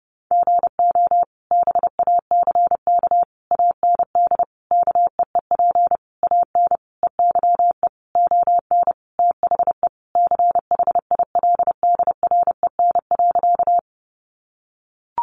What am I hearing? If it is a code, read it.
Morse code, heard as a keyed tone, about 30 wpm, 700 Hz. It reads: GO BACK AND KEEP AN EYE ON THE CHILDREN.
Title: Sentences from Top 300 Words 30wpm